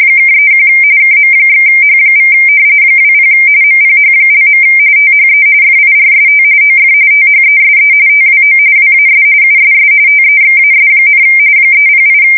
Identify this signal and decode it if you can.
RTTY, about 45.45 baud. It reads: (31 KM/H= 20 MPH). THE FIRST OF THESE TWO WAS USS TEXAS, 308
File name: fsk170 / battleships